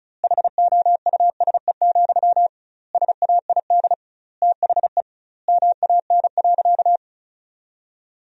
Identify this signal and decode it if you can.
Morse code, heard as a keyed tone, about 36 wpm, 700 Hz. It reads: HOUSE, SAID THE MAN.
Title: 2x Word-Spacing - Sentences from Top 400 Words - 35wpm